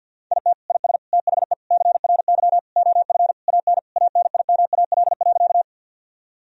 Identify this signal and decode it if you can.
Morse code, heard as a keyed tone, about 50 wpm, 700 Hz. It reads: IT IS THE CRY OF AN ANIMAL.
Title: Sentences from Top 1000 Words - Review Part 2 - 50wpm